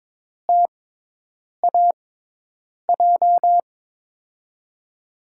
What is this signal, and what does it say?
Morse code, heard as a keyed tone, about 22 wpm, spaced as for 8 wpm, 700 Hz. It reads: TAJ
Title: USA Call Signs - 5 Characters - 6x Character Spacing - 22wpm